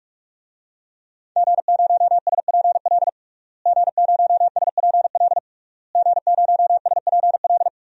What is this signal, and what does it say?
Morse code, heard as a keyed tone, about 46 wpm, 700 Hz. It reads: G0SPL G0SPL G0SPL
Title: DXCC-100 Call Signs - 5 Characters - Repeated 3x - 45wpm